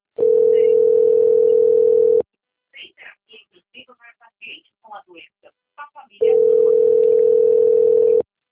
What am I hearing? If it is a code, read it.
Ringback tone (North America), 440 + 480 Hz, with speech.
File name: rg-1600-unknown- - - .26008